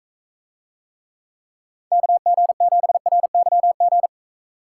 Morse code, heard as a keyed tone, about 41 wpm, 700 Hz. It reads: KG7RYG